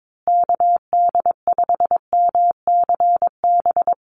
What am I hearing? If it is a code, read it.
Morse code, heard as a keyed tone, about 22 wpm, 700 Hz. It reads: KD5MCB
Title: USA Call Signs Repeated 3x 22wpm